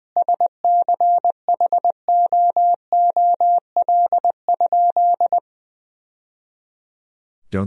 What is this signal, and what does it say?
Morse code, heard as a keyed tone, about 21 wpm, 700 Hz. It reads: SCHOOL?
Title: Sentences from Top 1000 Words - Review Part 1 - 20wpm